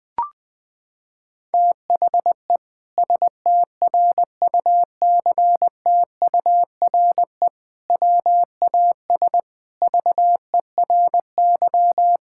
Morse code, heard as a keyed tone, about 21 wpm, 700 Hz. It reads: THE STRUCTURE WAS VERY
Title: Sentences from Top 1000 Words - Review Part 6 - 20wpm